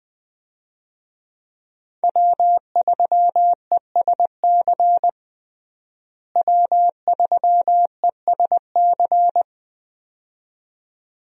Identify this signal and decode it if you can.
Morse code, heard as a keyed tone, about 20 wpm, 700 Hz. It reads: W3ESC W3ESC
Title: USA Call Signs - 5 Characters - Repeated 2x - 20wpm